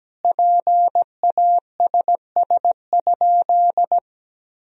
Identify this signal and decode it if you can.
Morse code, heard as a keyed tone, about 17 wpm, 700 Hz. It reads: PASS?
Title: Sentences from Top 400 Words and Abbreviations 17wpm